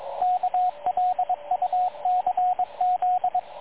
Morse code, heard as a keyed tone, about 22 wpm, 730 Hz, with radio noise. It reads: KLUCZ
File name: dlazaawansowanych / nagraniazpasma / websdr recording 3567.0kHz